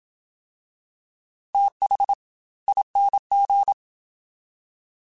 Morse code, heard as a keyed tone, about 27 wpm, 800 Hz. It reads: TH ING
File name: TH / TH mp3s / TH Prefixes spt @25